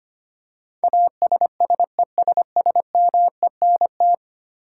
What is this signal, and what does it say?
Morse code, heard as a keyed tone, about 25 wpm, 700 Hz. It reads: ASSESSMENT
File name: 10 Letter Words - First 3 Letters Given - 25wpm